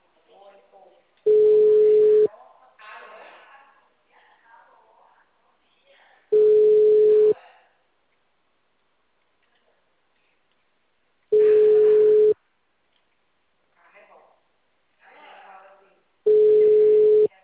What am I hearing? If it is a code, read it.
Ringback tone (Europe, 425 Hz), with speech.